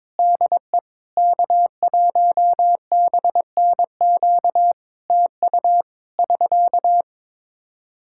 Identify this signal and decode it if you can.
Morse code, heard as a keyed tone, about 22 wpm, 700 Hz. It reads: DE K1BNQ TU <SK>
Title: Practice QSO - Exchange 4 - Ending - 22wpm